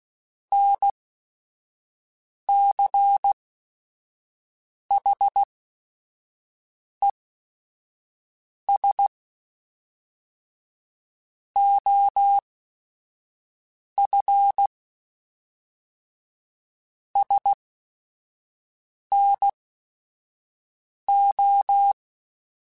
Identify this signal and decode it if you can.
Morse code, heard as a keyed tone, about 16 wpm, spaced as for 5 wpm, 800 Hz. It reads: NCHES OF SNO
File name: QSOs / QSOs-mp3s / WA7QOX 1@5